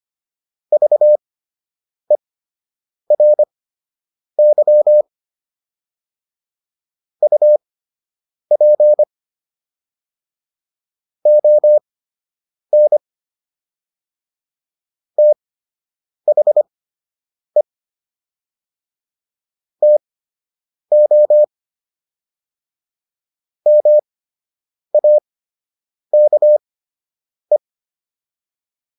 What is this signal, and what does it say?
Morse code, heard as a keyed tone, about 25 wpm, spaced as for 8 wpm, 600 Hz. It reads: VERY UP ON THE TO MAKE